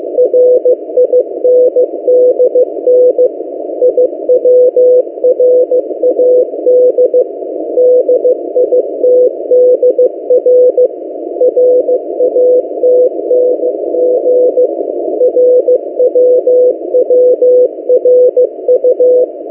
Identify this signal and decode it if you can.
Morse code, heard as a keyed tone, about 15 wpm, 500 Hz. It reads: RINDN IWRAD DITDR RATNG RWWRU
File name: M08a- -1355utc-8096khz